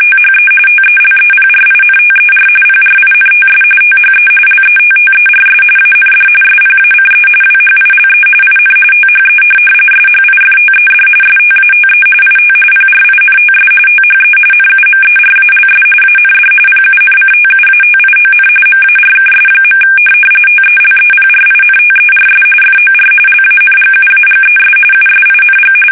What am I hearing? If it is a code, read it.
RTTY, about 45.45 baud. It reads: THE TELEPRINTER PRINTS THE LETTERS AND SPACE WHILE IN THE SHIFTED STATE IT PRINTS THE NUMERALS AND PUNCTUATION MARKS. TELEPRINTERS FOR LANGUAGES USING
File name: fsk850 / teletype-RTTY